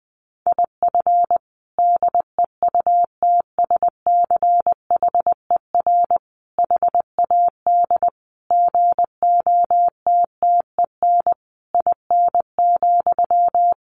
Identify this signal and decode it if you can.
Morse code, heard as a keyed tone, about 20 wpm, 700 Hz. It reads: IF DEUTSCHER HAD GOTTEN IN,
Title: A Sound Of Thunder 20wpm